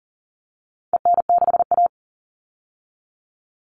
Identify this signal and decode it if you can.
Morse code, heard as a keyed tone, about 40 wpm, 700 Hz. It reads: EN6A
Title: All Cut Numbers - 4-Digit 40wpm